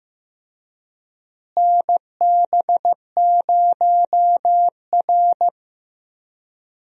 Morse code, heard as a keyed tone, about 15 wpm, 700 Hz. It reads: NB0R
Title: USA Call Signs - 4 Characters - Repeated 2x - 15wpm